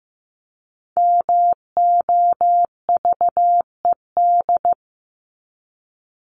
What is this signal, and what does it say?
Morse code, heard as a keyed tone, about 15 wpm, 700 Hz. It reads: MOVED
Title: ED Suffix - Top 100 Words with Base Words 15wpm